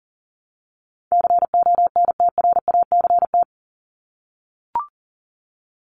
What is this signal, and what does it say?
Morse code, heard as a keyed tone, about 40 wpm, 700 Hz. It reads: CONTRACT
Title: Top 1251-1500 Words 40wpm